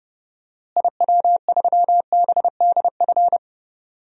Morse code, heard as a keyed tone, about 32 wpm, 700 Hz. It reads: IW3BDF
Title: DXCC-100 Call Signs - 6 Characters - Repeated 3x - 30wpm